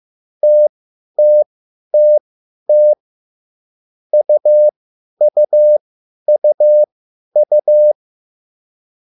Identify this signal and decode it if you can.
Morse code, heard as a keyed tone, about 15 wpm, spaced as for 10 wpm, 600 Hz. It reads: TTTT UUUU